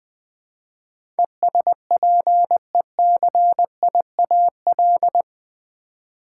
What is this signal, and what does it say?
Morse code, heard as a keyed tone, about 20 wpm, 700 Hz. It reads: ESPECIAL
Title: Build A Word - 4 Letters At A Time - 10-Letter Words - 20wpm